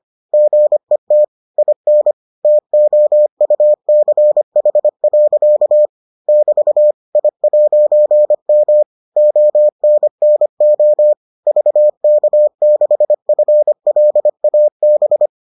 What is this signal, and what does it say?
Morse code, heard as a keyed tone, about 25 wpm, 600 Hz. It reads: GET IN TOUCH. <BT> I'M ONNO VK6FLAB